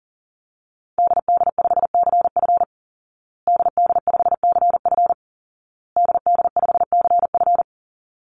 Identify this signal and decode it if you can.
Morse code, heard as a keyed tone, about 40 wpm, 700 Hz. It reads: DD5CF DD5CF DD5CF
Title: Call Signs Repeated 2x - 40wpm